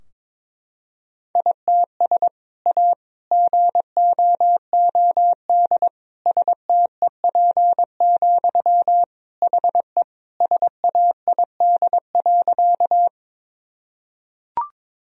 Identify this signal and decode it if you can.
Morse code, heard as a keyed tone, about 22 wpm, 700 Hz. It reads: ITS A GOOD STEP, HE SAID.
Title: Sentences from Top 1000 Words - Review Part 2 - 22wpm